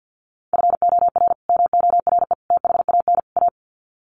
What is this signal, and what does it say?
Morse code, heard as a keyed tone, about 50 wpm, 700 Hz. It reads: FOR MORE THAN A
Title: Sets of 4 Words 50wpm